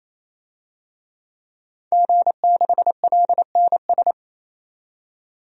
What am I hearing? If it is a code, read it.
Morse code, heard as a keyed tone, about 27 wpm, 700 Hz. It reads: G6LNS
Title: DXCC-100 Call Signs - 5 Characters - 28wpm